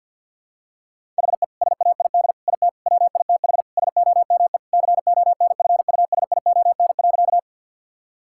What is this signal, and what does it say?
Morse code, heard as a keyed tone, about 50 wpm, 700 Hz. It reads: HE SAID IT WITH SOME CONFUSION.
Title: Sentences from Top 1600 Words 50wpm